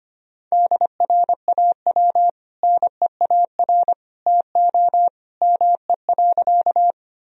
Morse code, heard as a keyed tone, about 25 wpm, 700 Hz. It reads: DRAW NEAR TO ME.
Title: Sentences from Top 1000 Words - Review Part 1 - 25wpm